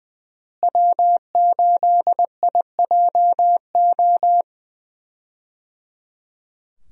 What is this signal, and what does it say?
Morse code, heard as a keyed tone, about 20 wpm, 700 Hz. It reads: W8IJO
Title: USA Call Signs - 5 Characters - Repeated 3x - 20wpm